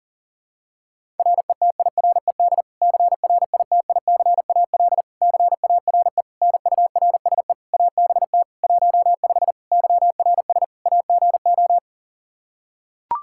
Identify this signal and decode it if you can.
Morse code, heard as a keyed tone, about 37 wpm, 700 Hz. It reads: RETIRED CRITICAL CARE NURSE ABT 15 YRS AGO
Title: Practice QSO - Exchange 3 - More Info - 40wpm